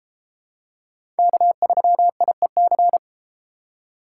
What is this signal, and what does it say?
Morse code, heard as a keyed tone, about 29 wpm, 700 Hz. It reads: K3IEC